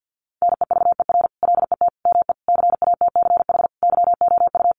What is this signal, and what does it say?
Morse code, heard as a keyed tone, about 50 wpm, 700 Hz. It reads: NEVER LET ME CATCH YOU
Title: Sentences from Top 1500 Words 50wpm